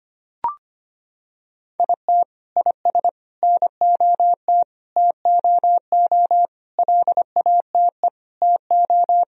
Morse code, heard as a keyed tone, about 25 wpm, 700 Hz. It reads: IT IS NOT TOO LATE TO
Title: Sentences from Top 1000 Words - Review Part 2 - 25wpm